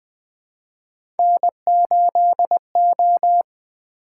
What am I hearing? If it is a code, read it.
Morse code, heard as a keyed tone, about 20 wpm, 700 Hz. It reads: N8O
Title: Three Letter-Number Combinations - 20wpm